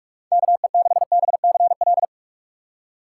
Morse code, heard as a keyed tone, about 46 wpm, 700 Hz. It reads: KE6BCL
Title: USA Call Signs - 6x Character Spacing - 45wpm